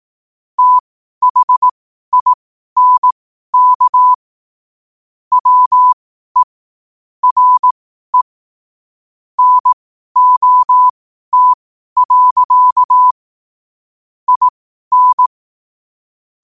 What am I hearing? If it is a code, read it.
Morse code, heard as a keyed tone, about 18 wpm, spaced as for 12 wpm, 1000 Hz. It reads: THINK WE RE NOT. IN
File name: qotdfiles / 12wpm